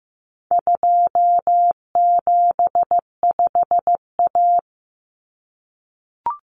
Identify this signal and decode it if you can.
Morse code, heard as a keyed tone, about 15 wpm, 700 Hz. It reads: 275A
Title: Common Cut Numbers - 4-Digit 15wpm